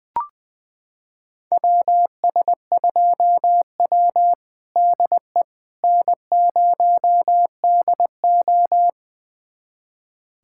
Morse code, heard as a keyed tone, about 21 wpm, 700 Hz. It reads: WS2W DE N0DO
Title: USA Callsign DE Callsign - 4 Characters - 20wpm